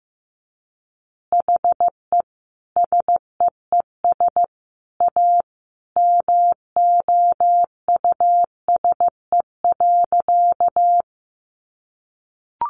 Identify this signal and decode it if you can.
Morse code, heard as a keyed tone, about 15 wpm, 700 Hz. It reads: HE SEES A MOUSE.
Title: Easy Phrases 15wpm